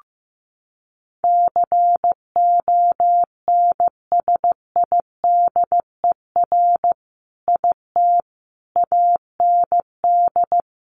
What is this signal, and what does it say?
Morse code, heard as a keyed tone, about 14 wpm, 700 Hz. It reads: CONSIDER IT AND
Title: Sentences from Top 1400 Words 15wpm